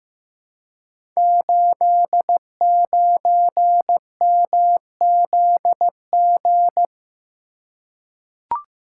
Morse code, heard as a keyed tone, about 15 wpm, 700 Hz. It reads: 89MZG
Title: Code Groups - 5 Letter:Numbers - 15wpm